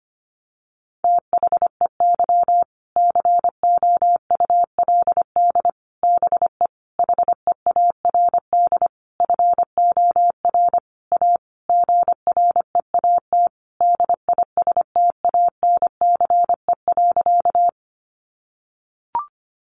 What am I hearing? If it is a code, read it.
Morse code, heard as a keyed tone, about 25 wpm, 700 Hz. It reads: THEY COULD BE HEARD FOR A GREAT DISTANCE.
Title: Sentences from Top 1200 Words 25wpm